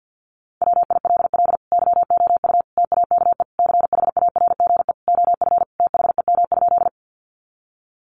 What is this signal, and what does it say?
Morse code, heard as a keyed tone, about 50 wpm, 700 Hz. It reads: WILL YOU TAKE CHARGE OF THEM?
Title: Sentences from Top 900 Words 50wpm